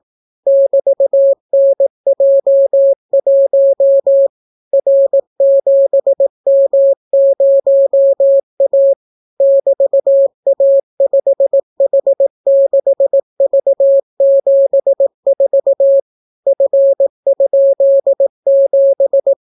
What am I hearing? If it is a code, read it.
Morse code, heard as a keyed tone, about 18 wpm, 550 Hz. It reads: <BT>NJ1 R7M0A <BT>A5H6V74 F?7